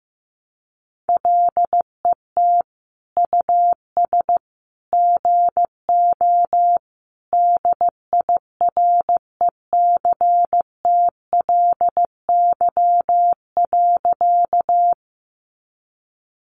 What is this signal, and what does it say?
Morse code, heard as a keyed tone, about 15 wpm, 700 Hz. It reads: LET US GO DIRECTLY.
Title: Sentences from Top 800 Words 15wpm